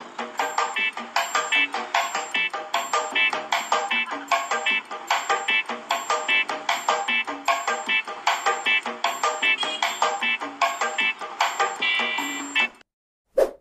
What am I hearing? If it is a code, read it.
There is no code.